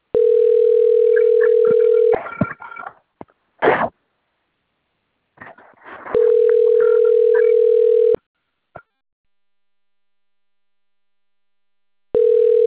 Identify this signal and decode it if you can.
Ringback tone (North America), 440 + 480 Hz.